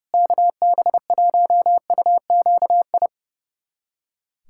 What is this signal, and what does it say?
Morse code, heard as a keyed tone, about 30 wpm, 700 Hz. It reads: KB1UQI
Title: USA Callsign DE Callsign 30wpm